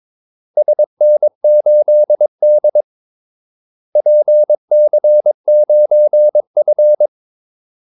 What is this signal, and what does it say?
Morse code, heard as a keyed tone, about 22 wpm, 600 Hz. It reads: SN8D PC9F
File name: Callsigns-22wpm